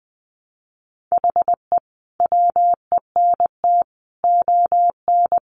Morse code, heard as a keyed tone, about 20 wpm, 700 Hz. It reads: HE WENT ON
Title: Sentences from Top 200 Words 20wpm